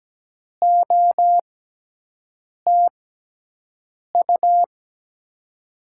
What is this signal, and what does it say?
Morse code, heard as a keyed tone, about 17 wpm, 700 Hz. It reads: O T U
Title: USA Call Signs - 6 Characters - 6x Character Spacing - 17wpm